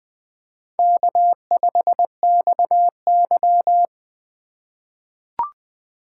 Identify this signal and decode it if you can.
Morse code, heard as a keyed tone, about 20 wpm, 700 Hz. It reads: K5XY
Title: USA Call Signs - 4 Characters - 4x Character Spacing - 20wpm